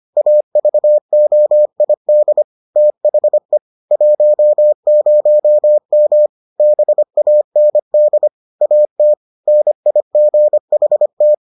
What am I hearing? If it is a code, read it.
Morse code, heard as a keyed tone, about 25 wpm, 600 Hz. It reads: AVOID THE 10M BAND AT NIGHT